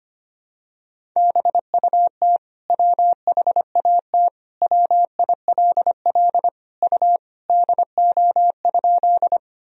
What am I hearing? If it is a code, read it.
Morse code, heard as a keyed tone, about 25 wpm, 700 Hz. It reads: BUT WHAT WILL U DO?
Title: Sentences from Top 100 Words and Abbreviations 25wpm